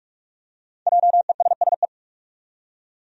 Morse code, heard as a keyed tone, about 47 wpm, 700 Hz. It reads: JESSE
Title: USA Common Names - Max 5 Letters - 45wpm